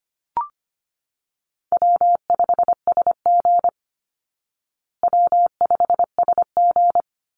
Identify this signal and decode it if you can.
Morse code, heard as a keyed tone, about 25 wpm, 700 Hz. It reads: W5SG W5SG
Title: Call Signs Repeated 3x - 25wpm